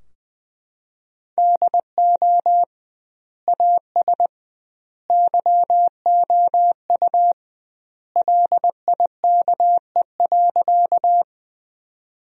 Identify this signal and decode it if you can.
Morse code, heard as a keyed tone, about 20 wpm, 700 Hz. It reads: DO AS YOU LIKE.
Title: 2x Word-Spacing - Sentences from Top 100 Words - 20wpm